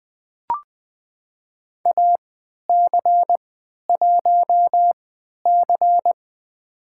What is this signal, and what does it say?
Morse code, heard as a keyed tone, about 20 wpm, spaced as for 11 wpm, 700 Hz. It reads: AC1C
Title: USA Call Signs - 4 Characters - 3x Character Spacing - 20wpm